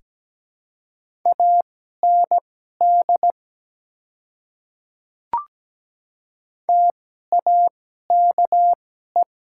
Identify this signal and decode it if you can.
Morse code, heard as a keyed tone, about 17 wpm, spaced as for 12 wpm, 700 Hz. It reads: AND TAKE
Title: 2x Character Spacing - Top 300 Words - 17wpm